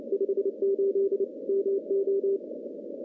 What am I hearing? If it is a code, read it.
Morse code, heard as a keyed tone, about 30 wpm, 390 Hz. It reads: 58 MO
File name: contests / CW-SS-k5tr- -0700